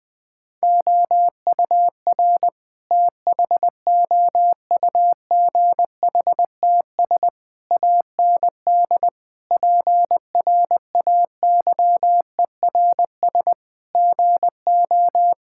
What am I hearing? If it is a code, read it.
Morse code, heard as a keyed tone, about 20 wpm, 700 Hz. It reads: OUR THOUGHTS AND PRAYERS GO
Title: Sets of 8 Words - 20wpm